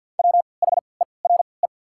Morse code, heard as a keyed tone, about 52 wpm, spaced as for 29 wpm, 690 Hz. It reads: WHERE
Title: 3x Character Spacing - Top 200 Words - 50wpm